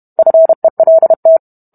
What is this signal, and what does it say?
Morse code, heard as a keyed tone, about 32 wpm, 650 Hz. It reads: FELT